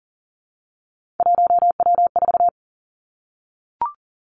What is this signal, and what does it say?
Morse code, heard as a keyed tone, about 42 wpm, 700 Hz. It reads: 1W4
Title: Three Challenging Letter-Number Combinations 40wpm